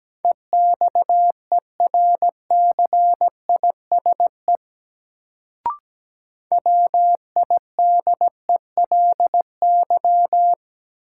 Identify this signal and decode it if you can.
Morse code, heard as a keyed tone, about 17 wpm, 700 Hz. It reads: EXERCISE WIDELY
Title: Sentences from Top 1200 Words - New Word Drill Alt - 17wpm